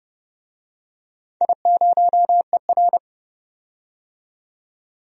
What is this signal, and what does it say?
Morse code, heard as a keyed tone, about 30 wpm, 700 Hz. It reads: I0ER
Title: DXCC-100 Call Signs - 4 Characters - 30wpm